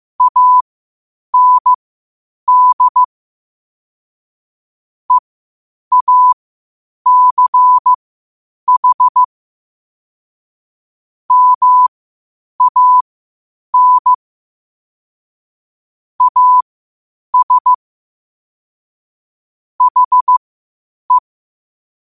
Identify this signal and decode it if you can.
Morse code, heard as a keyed tone, about 15 wpm, spaced as for 8 wpm, 1000 Hz. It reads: AND EACH MAN AS HE
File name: qotdfiles / 8wpm